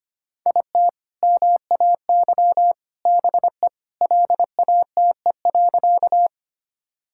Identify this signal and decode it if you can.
Morse code, heard as a keyed tone, about 25 wpm, 700 Hz. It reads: IT MAY BE LATE.
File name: practice / 25wpm / Sentences with 3-Word Phrases - 25wpm